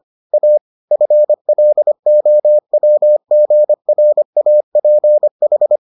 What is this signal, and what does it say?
Morse code, heard as a keyed tone, about 25 wpm, 600 Hz. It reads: A FLOWGRAPH